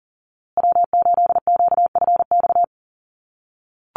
Morse code, heard as a keyed tone, about 39 wpm, 700 Hz. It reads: W8QFX
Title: USA Call Signs Repeated 2x 40wpm